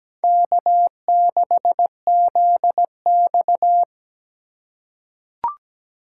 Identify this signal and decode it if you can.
Morse code, heard as a keyed tone, about 17 wpm, 700 Hz. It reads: K6ZX
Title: K1USN SST QSO Practice - 17wpm